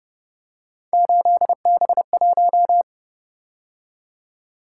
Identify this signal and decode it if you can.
Morse code, heard as a keyed tone, about 31 wpm, 700 Hz. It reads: 8B1